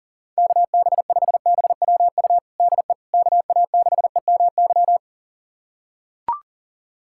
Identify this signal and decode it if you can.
Morse code, heard as a keyed tone, about 40 wpm, 690 Hz. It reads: KB5BWU DE KA6EMY
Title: USA Callsign DE Callsign - 6 Characters - 40wpm